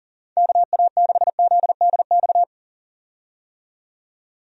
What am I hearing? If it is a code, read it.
Morse code, heard as a keyed tone, about 40 wpm, 700 Hz. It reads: KA6ZDX
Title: USA Call Signs - 6 Characters - 40wpm